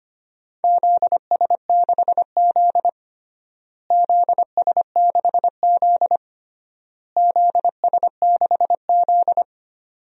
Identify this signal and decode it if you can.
Morse code, heard as a keyed tone, about 25 wpm, 700 Hz. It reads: ZS6Z ZS6Z ZS6Z
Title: DXCC-100 Call Signs - 4 Characters - Repeated 3x - 25wpm